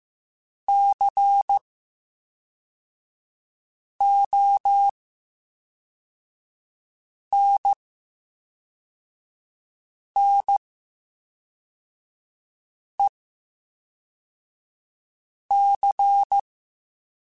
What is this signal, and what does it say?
Morse code, heard as a keyed tone, about 15 wpm, 800 Hz. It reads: C O N N E C
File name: Special INGs / Special INGs-mp3s / VerySlowToFaster Ing @10